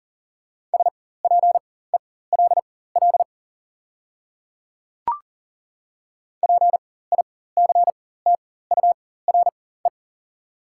Morse code, heard as a keyed tone, about 40 wpm, spaced as for 19 wpm, 690 Hz. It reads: SPELL PICTURE
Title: 4x Character Spacing - Top 200 Words - 40wpm